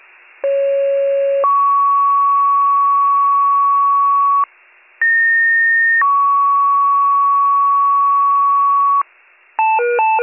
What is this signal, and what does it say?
Two-tone page, heard as a one-second tone, then a three-second tone, about 550 then 1120 Hz, about 1850 then 1120 Hz. It is heard from 0.4 to 9.0 s.